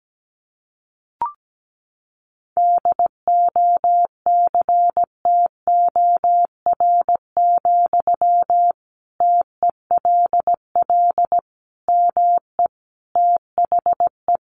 Morse code, heard as a keyed tone, about 17 wpm, 700 Hz. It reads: DOCTOR, TELL ME THE
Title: Sentences from Top 1500 Words 17wpm